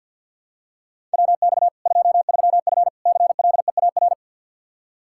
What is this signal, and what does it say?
Morse code, heard as a keyed tone, about 46 wpm, 700 Hz. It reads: WX 23F CLEAR
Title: K1USN SST QSO Practice 50wpm